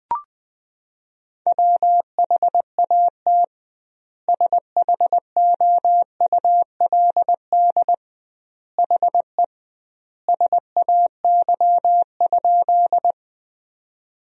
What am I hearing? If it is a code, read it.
Morse code, heard as a keyed tone, about 21 wpm, 700 Hz. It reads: WHAT SHOULD HE SAY?
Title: 2x Word-Spacing - Sentences from Top 300 Words - 20wpm